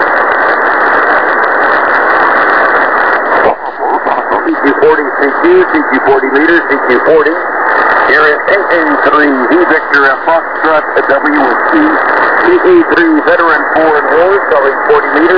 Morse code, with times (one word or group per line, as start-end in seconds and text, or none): none